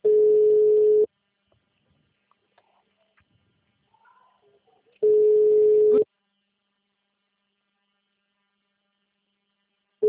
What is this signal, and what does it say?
Ringback tone (Europe, 425 Hz).
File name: out- -2013- - - .73